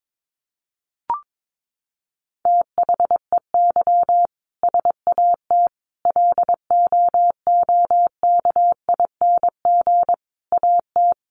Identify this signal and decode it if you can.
Morse code, heard as a keyed tone, about 22 wpm, 700 Hz. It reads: THEY SAT LOOKING AT
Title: A Sound Of Thunder 22wpm